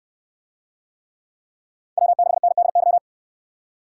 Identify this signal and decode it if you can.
Morse code, heard as a keyed tone, about 69 wpm, 700 Hz. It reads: W6NDY